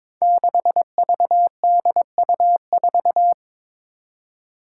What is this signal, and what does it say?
Morse code, heard as a keyed tone, about 22 wpm, 700 Hz. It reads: 6VDU4
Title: Five Challenging Letter-Number Combinations 22wpm